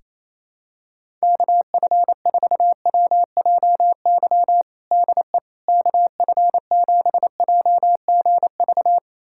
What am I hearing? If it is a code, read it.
Morse code, heard as a keyed tone, about 28 wpm, 700 Hz. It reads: KF4WJY DE KF7JGV
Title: USA Callsign DE Callsign 28wpm